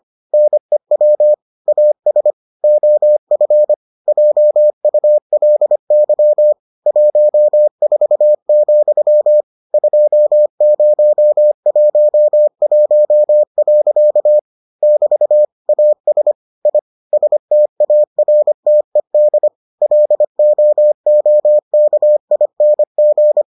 Morse code, heard as a keyed tone, about 25 wpm, 600 Hz. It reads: NEW AS OF JULY 14, 2011. <BT> AS I STARTED LOOKING